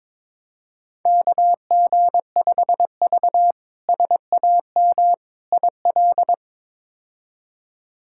Morse code, heard as a keyed tone, about 22 wpm, 700 Hz. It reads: KG5V SAM IL
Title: K1USN SST QSO Practice - 22wpm